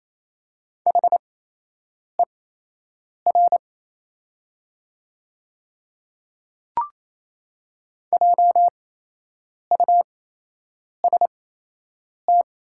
Morse code, heard as a keyed tone, about 28 wpm, spaced as for 8 wpm, 700 Hz. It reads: HER JUST
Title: 8x Character Spacing - Top 200 Words - 28wpm